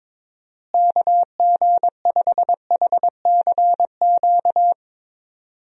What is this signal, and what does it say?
Morse code, heard as a keyed tone, about 22 wpm, 700 Hz. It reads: KG5HCQ
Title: USA Call Signs - 6 Characters - Repeated 2x - 22wpm